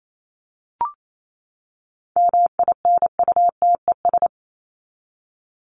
Morse code, heard as a keyed tone, about 28 wpm, 700 Hz. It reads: MINUTES
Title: ES:S Suffix - Top 100 Words with Base Words 28wpm